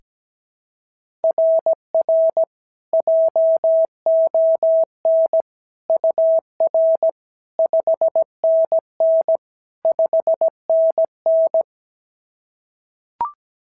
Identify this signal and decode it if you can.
Morse code, heard as a keyed tone, about 17 wpm, 690 Hz. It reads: RR JON UR 5NN 5NN
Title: SOTA QSO Practice 17wpm